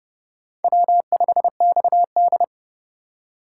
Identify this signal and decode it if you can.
Morse code, heard as a keyed tone, about 28 wpm, 700 Hz. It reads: W5XD